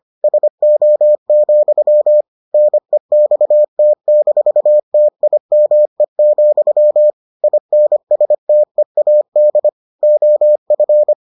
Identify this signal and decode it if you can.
Morse code, heard as a keyed tone, about 25 wpm, 600 Hz. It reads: SO, NEXT-TIME, INSTEAD OF